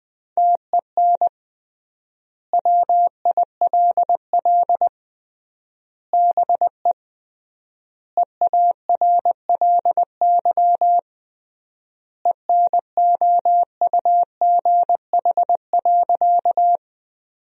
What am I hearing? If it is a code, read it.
Morse code, heard as a keyed tone, about 20 wpm, 700 Hz. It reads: TEN WILL BE EARLY ENOUGH.
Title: 3x Word-Spacing - Sentences from Top 400 Words - 20wpm